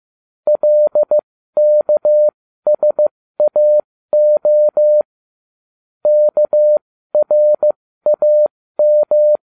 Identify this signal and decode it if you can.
Morse code, heard as a keyed tone, about 15 wpm, spaced as for 13 wpm, 600 Hz. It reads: LKSAO KRAM